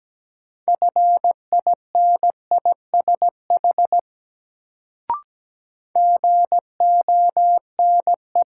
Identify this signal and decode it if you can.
Morse code, heard as a keyed tone, about 17 wpm, 700 Hz. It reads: FINISH GONE
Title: Sentences from Top 1200 Words - New Word Drill Alt - 17wpm